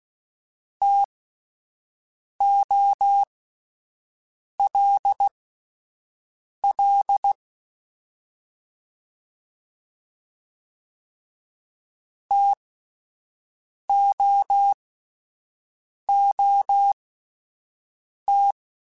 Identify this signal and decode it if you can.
Morse code, heard as a keyed tone, about 16 wpm, spaced as for 5 wpm, 800 Hz. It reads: TOLL TOOT
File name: Spaced Alphas / Spaced Alphas mp3s / Ts sp @10